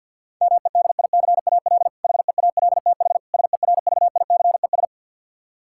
Morse code, heard as a keyed tone, about 51 wpm, 700 Hz. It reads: MEDICAL HEALTH SERVICES
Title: K1USN SST QSO Practice 50wpm